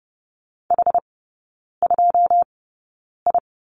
Morse code, heard as a keyed tone, about 30 wpm, spaced as for 9 wpm, 700 Hz. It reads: H2I
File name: 081 Lesson 14e - Number 2 - Three Characters 30wpm